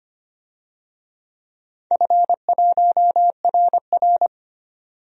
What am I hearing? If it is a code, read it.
Morse code, heard as a keyed tone, about 25 wpm, 700 Hz. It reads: F1RR
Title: DXCC-100 Call Signs - 4 Characters - 25wpm